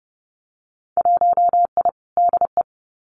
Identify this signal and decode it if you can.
Morse code, heard as a keyed tone, about 30 wpm, 700 Hz. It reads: 1I DE
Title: Winter Field Day Exchanges Encore 30wpm